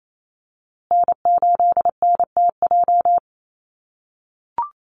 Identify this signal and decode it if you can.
Morse code, heard as a keyed tone, about 28 wpm, 700 Hz. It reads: N8NTJ
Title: USA Call Signs Repeated 3x 28wpm